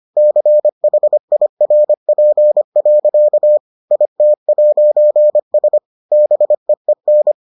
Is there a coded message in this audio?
Yes: Morse code from 0.2 to 7.3 s, heard as a keyed tone, about 25 wpm, 600 Hz. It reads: CHIRP. IT'S BEEN